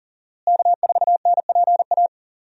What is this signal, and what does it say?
Morse code, heard as a keyed tone, about 40 wpm, 700 Hz. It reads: K4NPA
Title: USA Call Signs - 5 Characters - Repeated 3x - 40wpm